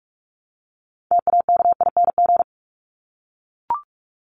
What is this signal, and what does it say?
Morse code, heard as a keyed tone, about 44 wpm, 700 Hz. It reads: TAKING
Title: ING Suffix - Top 100 Words with Base Words 45wpm